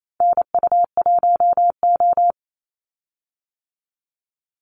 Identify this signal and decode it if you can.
Morse code, heard as a keyed tone, about 28 wpm, 700 Hz. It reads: NU1O
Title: Call Signs - 28wpm